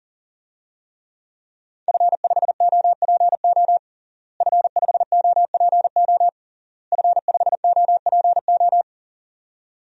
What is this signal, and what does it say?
Morse code, heard as a keyed tone, about 40 wpm, 700 Hz. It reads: F5OPO F5OPO F5OPO
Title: DXCC-100 Call Signs - 5 Characters - Repeated 3x - 40wpm